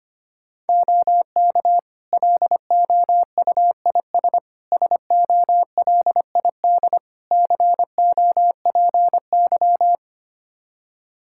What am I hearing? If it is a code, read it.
Morse code, heard as a keyed tone, about 25 wpm, 700 Hz. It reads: OK LOUIS SOLID COPY
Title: Practice QSO - Exchange 3 - More Info - 25wpm